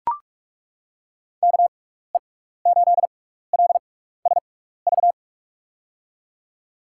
Morse code, heard as a keyed tone, about 41 wpm, spaced as for 15 wpm, 700 Hz. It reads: KE8LSV
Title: USA Call Signs - 6 Characters - 6x Character Spacing - 45wpm